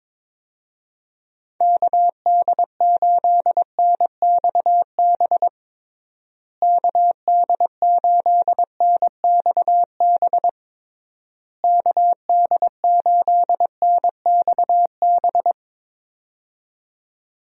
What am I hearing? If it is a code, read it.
Morse code, heard as a keyed tone, about 22 wpm, 700 Hz. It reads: KD8NXB KD8NXB KD8NXB
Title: USA Call Signs - 6 Characters - Repeated 3x - 22wpm